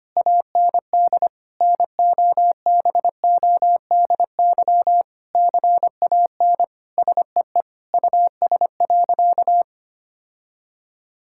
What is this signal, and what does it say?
Morse code, heard as a keyed tone, about 25 wpm, 700 Hz. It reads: AND NOBODY CAN SEE US.
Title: Sentences from Top 1600 Words 25wpm